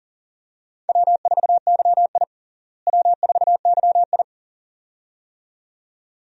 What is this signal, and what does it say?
Morse code, heard as a keyed tone, about 40 wpm, 700 Hz. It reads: W4YI W4YI
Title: USA Call Signs - 4 Characters - Repeated 2x - 40wpm